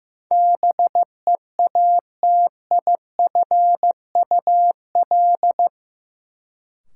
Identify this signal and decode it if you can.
Morse code, heard as a keyed tone, about 15 wpm, 700 Hz. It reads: BEATIFUL
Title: Sentences from Top 1300 Words - New Word Drill - 15wpm